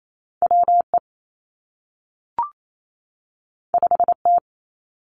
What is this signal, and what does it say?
Morse code, heard as a keyed tone, about 28 wpm, 700 Hz. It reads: W E 5 T
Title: Two Challenging Letter-Number Combinations 28wpm